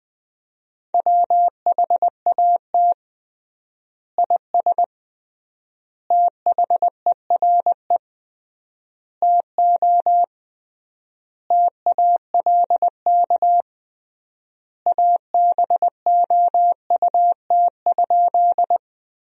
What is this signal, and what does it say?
Morse code, heard as a keyed tone, about 20 wpm, 700 Hz. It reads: WHAT IS THERE TO TALK ABOUT?
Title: 3x Word-Spacing - Sentences from Top 400 Words - 20wpm